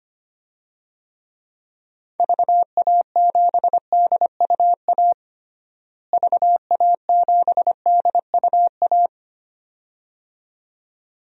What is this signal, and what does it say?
Morse code, heard as a keyed tone, about 25 wpm, 700 Hz. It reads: VA7DUA VA7DUA
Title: DXCC-100 Call Signs - 6 Characters - Repeated 2x - 25wpm